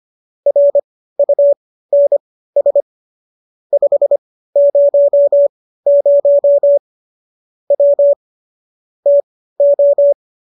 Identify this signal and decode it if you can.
Morse code, heard as a keyed tone, about 25 wpm, spaced as for 15 wpm, 570 Hz. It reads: RUNS 500 W TO